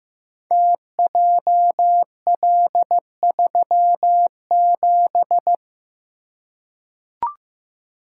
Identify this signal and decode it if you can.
Morse code, heard as a keyed tone, about 15 wpm, 700 Hz. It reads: TJL37
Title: Code Groups - 5 Letter:Numbers - 15wpm